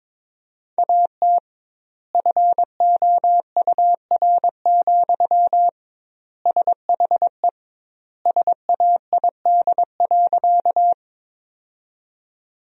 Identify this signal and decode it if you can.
Morse code, heard as a keyed tone, about 22 wpm, 700 Hz. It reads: AT FOUR, SHE SAID.
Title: 2x Word-Spacing - Sentences from Top 300 Words - 22wpm